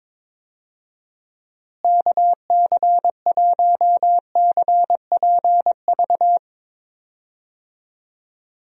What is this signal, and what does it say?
Morse code, heard as a keyed tone, about 21 wpm, 700 Hz. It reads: KC1CPV
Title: USA Call Signs - 6 Characters - Repeated 3x - 22wpm